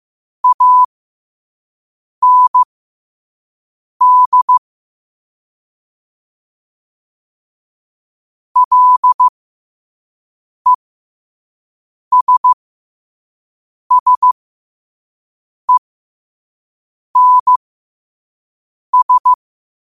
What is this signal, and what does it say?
Morse code, heard as a keyed tone, about 15 wpm, spaced as for 5 wpm, 1000 Hz. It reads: AND LESSENS
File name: qotdfiles / 5wpm